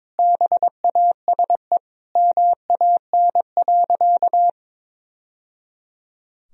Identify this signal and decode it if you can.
Morse code, heard as a keyed tone, about 21 wpm, 700 Hz. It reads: BASE MAN.
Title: Sentences from Top 1000 Words - Review Part 1 - 22wpm